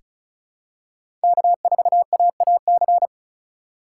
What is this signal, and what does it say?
Morse code, heard as a keyed tone, about 34 wpm, 700 Hz. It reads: K4AAC
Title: K1USN SST QSO Practice - 35wpm